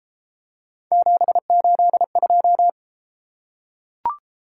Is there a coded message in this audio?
Yes: Morse code from 0.9 to 2.7 s, heard as a keyed tone, about 31 wpm, 700 Hz. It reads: 782